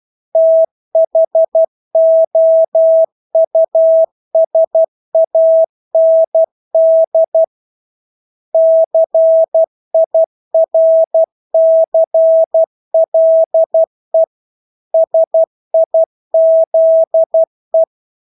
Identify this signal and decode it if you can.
Morse code, heard as a keyed tone, about 12 wpm, 650 Hz. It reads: THOUSAND CIRCLE SIZE
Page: Engelske ord 12 wpm